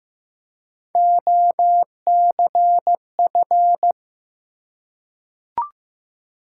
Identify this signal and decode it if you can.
Morse code, heard as a keyed tone, about 15 wpm, 700 Hz. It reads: OCF
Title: Top 180 QSO Elements 15wpm